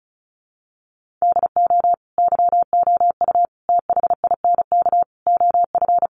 Morse code, heard as a keyed tone, about 34 wpm, 700 Hz. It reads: DO YOU THINK OF
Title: Sets of 5 Words Encore2 35wpm